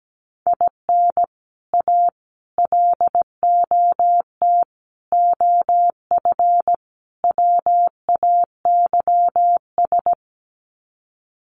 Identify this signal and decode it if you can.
Morse code, heard as a keyed tone, about 17 wpm, 700 Hz. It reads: IN A LOT OF WAYS
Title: Sets of 5 Words Encore 17wpm